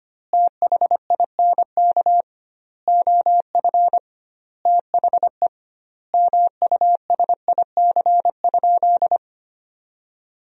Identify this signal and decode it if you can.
Morse code, heard as a keyed tone, about 25 wpm, 700 Hz. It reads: THINK OF THE MUSIC?
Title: 2x Word-Spacing - Sentences from Top 300 Words - 25wpm